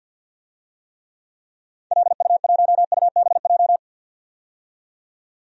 Morse code, heard as a keyed tone, about 46 wpm, 700 Hz. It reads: LU1UBJ
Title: DXCC-100 Call Signs - 6 Characters - 50wpm